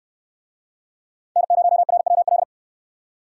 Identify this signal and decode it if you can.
Morse code, heard as a keyed tone, about 60 wpm, 700 Hz. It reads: N0RWL